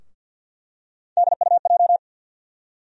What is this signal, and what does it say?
Morse code, heard as a keyed tone, about 51 wpm, 700 Hz. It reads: DUJ
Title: USA Call Sign Postfixes - 3 Letters 50wpm